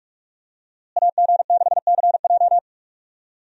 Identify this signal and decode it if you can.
Morse code, heard as a keyed tone, about 46 wpm, 700 Hz. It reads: AG6CJ
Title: USA Call Signs - 5 Characters - Repeated 3x - 45wpm